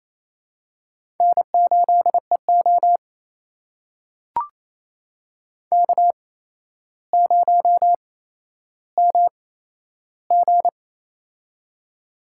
Morse code, heard as a keyed tone, about 28 wpm, spaced as for 12 wpm, 700 Hz. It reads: N8EO K0MG
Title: USA Call Signs - 4 Characters - 8x Character Spacing - 28wpm